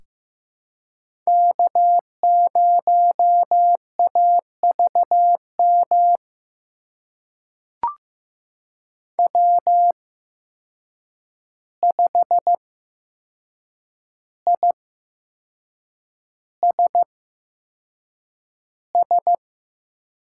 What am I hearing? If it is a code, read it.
Morse code, heard as a keyed tone, about 15 wpm, spaced as for 6 wpm, 700 Hz. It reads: K0AVM W5ISS
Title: USA Call Signs - 5 Characters - 8x Character Spacing - 15wpm